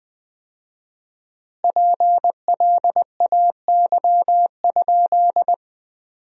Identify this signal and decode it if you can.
Morse code, heard as a keyed tone, about 20 wpm, 700 Hz. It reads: PLAY?
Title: 4x Word-Spacing - Sentences from Top 200 Words - 20wpm